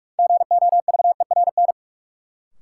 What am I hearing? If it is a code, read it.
Morse code, heard as a keyed tone, about 41 wpm, 700 Hz. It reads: GOVERN
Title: Sentences from Top 1000 Words - Review Part 2 - New Word Drill - 45wpm